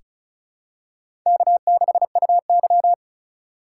Morse code, heard as a keyed tone, about 36 wpm, 700 Hz. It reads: K6UY
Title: USA Call Signs - 4 Characters - 2x Character Spacing - 35wpm